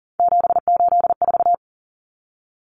Morse code, heard as a keyed tone, about 40 wpm, 700 Hz. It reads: 784
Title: Common Cut Numbers - 3-Digit 40wpm